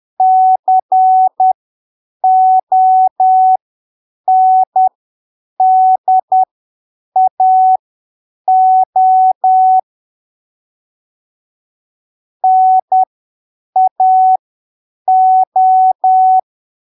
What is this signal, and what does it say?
Morse code, heard as a keyed tone, about 10 wpm, spaced as for 7 wpm, 750 Hz. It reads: CONDAO NAO